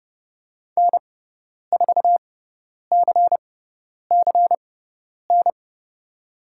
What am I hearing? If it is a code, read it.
Morse code, heard as a keyed tone, about 30 wpm, spaced as for 10 wpm, 700 Hz. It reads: N4CCN
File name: Lesson 17g - Letter u - Call Signs 30wpm